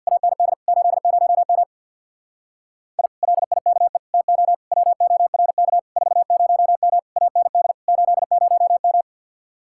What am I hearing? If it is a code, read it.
Morse code, heard as a keyed tone, about 64 wpm, 700 Hz. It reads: AND 80M I LIKE TO WORK 40M AND 80M